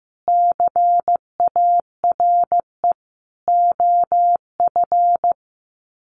Morse code, heard as a keyed tone, about 15 wpm, 700 Hz. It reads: CARE OF
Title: Sets of 4 Words 15wpm